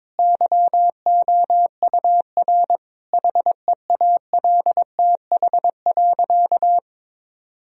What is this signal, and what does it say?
Morse code, heard as a keyed tone, about 22 wpm, 700 Hz. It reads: YOUR HEALTH.
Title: Sentences from Top 1000 Words - Review Part 3 - 22wpm